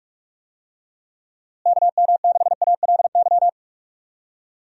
Morse code, heard as a keyed tone, about 46 wpm, 700 Hz. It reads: KM6ALY
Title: USA Call Signs - 6 Characters - Repeated 2x - 45wpm